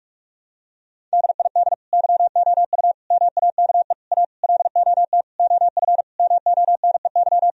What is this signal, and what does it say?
Morse code, heard as a keyed tone, about 46 wpm, 700 Hz. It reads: DID YOU MAKE A LOT OF MONEY
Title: Sentences from Top 1000 Words - Review Part 4 - 45wpm